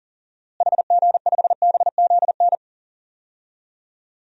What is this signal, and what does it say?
Morse code, heard as a keyed tone, about 40 wpm, 700 Hz. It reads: HG5BZN
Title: DXCC-100 Call Signs - 6 Characters - Repeated 3x - 40wpm